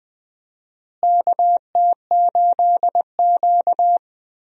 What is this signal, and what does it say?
Morse code, heard as a keyed tone, about 21 wpm, 700 Hz. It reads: KT8Q
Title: USA Call Signs - 4 Characters - Repeated 2x - 20wpm